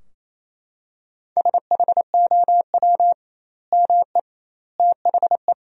Morse code, heard as a keyed tone, about 28 wpm, 700 Hz. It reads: SHOW ME THE
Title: 2x Word-Spacing - Sentences from Top 200 Words - 28wpm